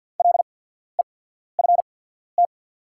Morse code, heard as a keyed tone, about 48 wpm, spaced as for 14 wpm, 690 Hz. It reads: LEFT
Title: 8x Character Spacing - Top 500 Words - 50wpm